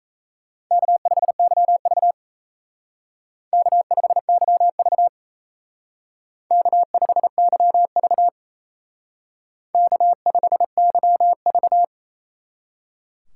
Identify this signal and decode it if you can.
Morse code, heard as a keyed tone, about 35 wpm, 700 Hz. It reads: K5YV K5YV K5YV K5YV
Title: Speed Racing - USA 1x2 Call Signs 28wpm